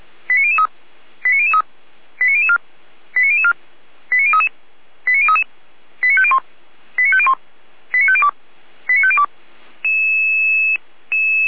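Five-tone selective call (ZVEI), heard as quick runs of five tones, tones of about 80 ms. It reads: 89003 89003 89004 89004 89033 89033 89501 89501 89502 89502
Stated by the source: ZVEI five-tone selective call 89099, received via SDR by feuerwehr-orlamuende.org.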